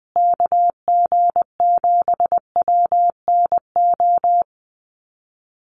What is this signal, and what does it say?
Morse code, heard as a keyed tone, about 20 wpm, 700 Hz. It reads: KG7WNO
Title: USA Call Signs Repeated 2x 20wpm